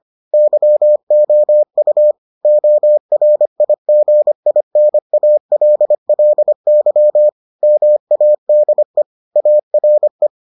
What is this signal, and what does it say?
Morse code, heard as a keyed tone, about 25 wpm, 600 Hz. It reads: YOU ORIGINALLY MADE ARE